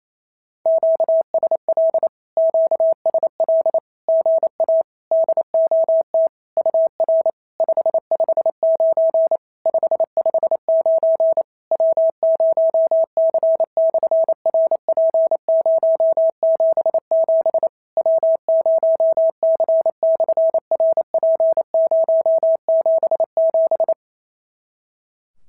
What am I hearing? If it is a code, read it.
Morse code, heard as a keyed tone, about 28 wpm, 650 Hz. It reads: QSL QSL GA DOT UR 559 559 W0C/RP077 W0C/RP077
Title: SOTA QSO Practice 28wpm